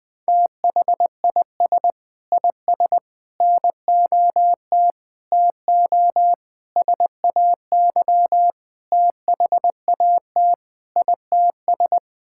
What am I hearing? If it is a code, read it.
Morse code, heard as a keyed tone, about 21 wpm, 700 Hz. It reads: THIS IS NOT TO SAY THAT ITS
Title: Sentences with 6-Word Phrases - 20wpm